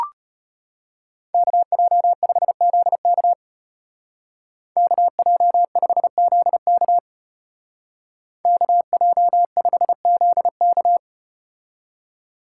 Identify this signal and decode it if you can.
Morse code, heard as a keyed tone, about 35 wpm, 700 Hz. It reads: KJ5ZK KJ5ZK KJ5ZK